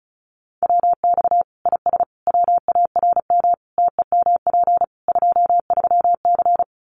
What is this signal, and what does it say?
Morse code, heard as a keyed tone, about 34 wpm, 700 Hz. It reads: WX IS WARM TEMP 23C
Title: Common QSO Phrases 35wpm